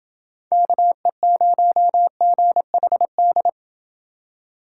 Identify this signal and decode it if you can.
Morse code, heard as a keyed tone, about 28 wpm, 700 Hz. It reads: KE0GHD